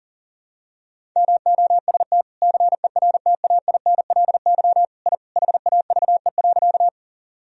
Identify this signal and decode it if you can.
Morse code, heard as a keyed tone, about 40 wpm, 700 Hz. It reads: MOST CERTAINLY I HAVE.
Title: Sentences from Top 1000 Words - Review Part 1 - 40wpm